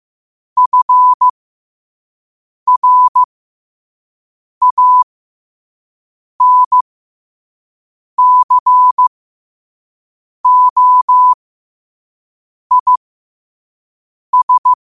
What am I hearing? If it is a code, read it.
Morse code, heard as a keyed tone, about 15 wpm, spaced as for 5 wpm, 1000 Hz. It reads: FRANCOIS
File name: qotdfiles / 5wpm